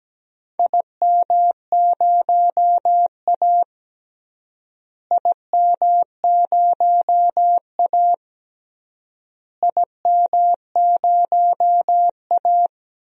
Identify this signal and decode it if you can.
Morse code, heard as a keyed tone, about 17 wpm, 700 Hz. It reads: IM0A IM0A IM0A
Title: DXCC-100 Call Signs - 4 Characters - Repeated 3x - 17wpm